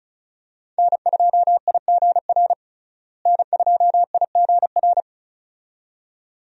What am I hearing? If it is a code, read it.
Morse code, heard as a keyed tone, about 36 wpm, 700 Hz. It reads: N2IGR N2IGR
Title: USA Call Signs - 5 Characters - Repeated 2x - 35wpm